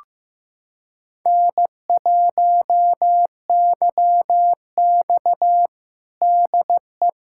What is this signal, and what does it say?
Morse code, heard as a keyed tone, about 15 wpm, 700 Hz. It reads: N1YX DE
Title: USA Callsign DE Callsign - 4 Characters - 15wpm